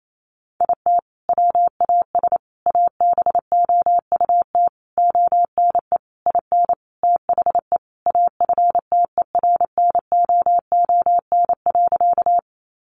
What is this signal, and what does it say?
Morse code, heard as a keyed tone, about 28 wpm, 700 Hz. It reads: IT WAS ABOUT ONE IN THE AFTERNOON.
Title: Sentences from Top 1500 Words 28wpm